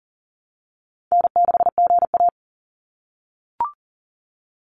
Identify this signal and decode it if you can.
Morse code, heard as a keyed tone, about 40 wpm, 700 Hz. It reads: N6GA
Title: Easy Call Signs - 40wpm